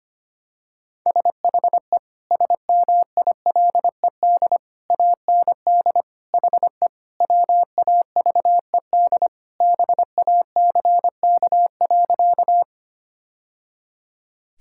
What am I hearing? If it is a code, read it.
Morse code, heard as a keyed tone, about 25 wpm, 690 Hz. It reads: SHE SMILED AND HE WAVED BACK.
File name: practice / 25wpm / Sentences with 2-Word Phrases - 25wpm